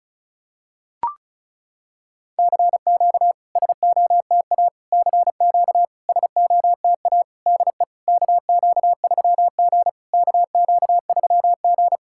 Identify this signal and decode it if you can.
Morse code, heard as a keyed tone, about 36 wpm, 690 Hz. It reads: CQ SOTA CQ SOTA DE KQ3G KQ3G
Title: SOTA QSO Practice 35wpm